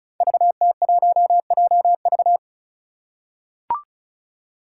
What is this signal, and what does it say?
Morse code, heard as a keyed tone, about 36 wpm, 700 Hz. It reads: VT1JV
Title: Five Challenging Letter-Number Combinations 35wpm